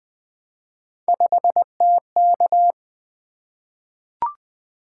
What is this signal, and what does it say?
Morse code, heard as a keyed tone, about 20 wpm, 700 Hz. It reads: 5TK
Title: Three Letter-Number Combinations - 20wpm